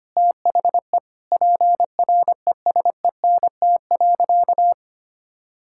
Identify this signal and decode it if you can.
Morse code, heard as a keyed tone, about 25 wpm, 700 Hz. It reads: THE PRESENT.
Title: Sentences from Top 1000 Words - Review Part 5 - 25wpm